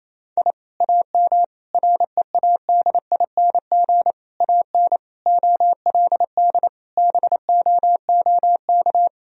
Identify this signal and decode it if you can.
Morse code, heard as a keyed tone, about 28 wpm, 700 Hz. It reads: I AM READING AN OLD BOOK
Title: Sentences from Top 1000 Words - Review Part 4 - 28wpm